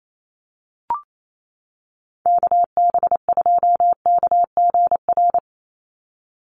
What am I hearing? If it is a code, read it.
Morse code, heard as a keyed tone, about 28 wpm, 700 Hz. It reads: KB2KGR
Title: USA Call Signs 28wpm